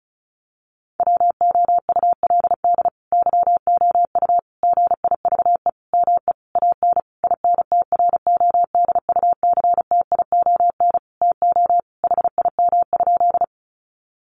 Morse code, heard as a keyed tone, about 34 wpm, 700 Hz. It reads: WOULD YOU GIVE ME AN INTRODUCTION TO HIM?
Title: Sentences from Top 1000 Words 35wpm